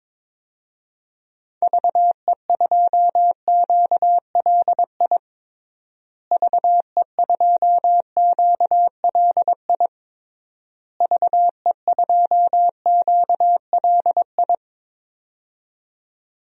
Morse code, heard as a keyed tone, about 22 wpm, 700 Hz. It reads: VE2QLI VE2QLI VE2QLI
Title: DXCC-100 Call Signs - 6 Characters - Repeated 3x - 22wpm